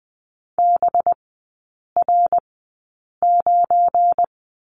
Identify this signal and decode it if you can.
Morse code, heard as a keyed tone, about 20 wpm, spaced as for 8 wpm, 700 Hz. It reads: BR9
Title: 184 Lesson 30e - Letter b - Three Characters 20wpm